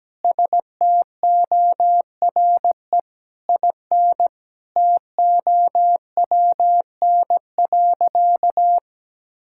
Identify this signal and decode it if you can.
Morse code, heard as a keyed tone, about 17 wpm, 700 Hz. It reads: STORE IN TOWN.
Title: Sentences from Top 1000 Words - Review Part 5 - 17wpm